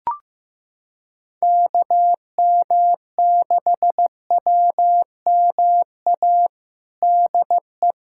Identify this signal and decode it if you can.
Morse code, heard as a keyed tone, about 16 wpm, 700 Hz. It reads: KM6WMA DE
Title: USA Callsign DE Callsign - 6 Characters - 15wpm